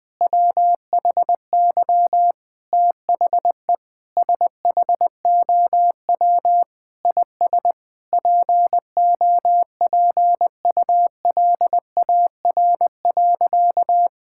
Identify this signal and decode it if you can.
Morse code, heard as a keyed tone, about 20 wpm, 700 Hz. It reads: WHY THE SHOW IS POPULAR.
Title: Sentences with 6-Word Phrases - Spoken Beginning - 20wpm